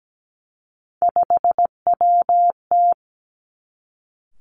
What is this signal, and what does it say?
Morse code, heard as a keyed tone, about 17 wpm, 700 Hz. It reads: 5WT
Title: Three Challenging Letter-Number Combinations 17wpm